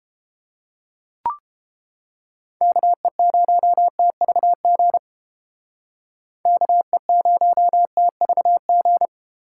Morse code, heard as a keyed tone, about 30 wpm, 700 Hz. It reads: KE0TVG KE0TVG